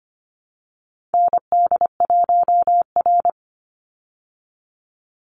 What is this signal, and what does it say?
Morse code, heard as a keyed tone, about 25 wpm, 700 Hz. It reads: ND1R
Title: Call Signs Repeated 2x - 25wpm